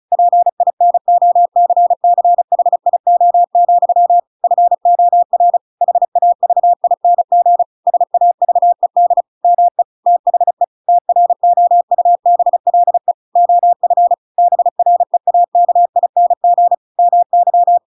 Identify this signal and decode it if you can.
Morse code, heard as a keyed tone, about 34 wpm, 700 Hz. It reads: PINOCCHIO, FOR HAVING SAVED ME THE TROUBLE OF BREAKING MY